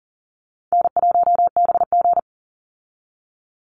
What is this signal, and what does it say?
Morse code, heard as a keyed tone, about 40 wpm, 700 Hz. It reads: N1BG
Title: Call Signs Repeated 2x - 40wpm